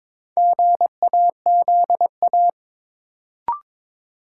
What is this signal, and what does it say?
Morse code, heard as a keyed tone, about 21 wpm, 700 Hz. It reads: GAZA
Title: Sets of 5 Words - New Word Drill - 22wpm